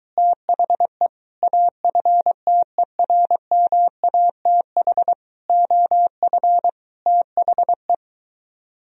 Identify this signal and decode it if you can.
Morse code, heard as a keyed tone, about 23 wpm, 700 Hz. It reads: THE AFTERMATH OF THE